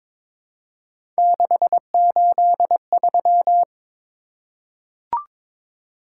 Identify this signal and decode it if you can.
Morse code, heard as a keyed tone, about 22 wpm, 700 Hz. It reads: 683
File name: practice / 22wpm / Three Letter-Number Combinations - 22wpm